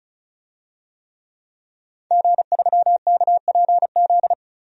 Morse code, heard as a keyed tone, about 35 wpm, 700 Hz. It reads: G3KPZ
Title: DXCC-100 Call Signs - 5 Characters - 35wpm